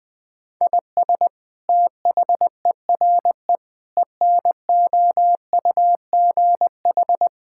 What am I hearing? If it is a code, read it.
Morse code, heard as a keyed tone, about 21 wpm, 700 Hz. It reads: IS THERE ENOUGH
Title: Sentences from Top 1000 Words - Review Part 5 - 20wpm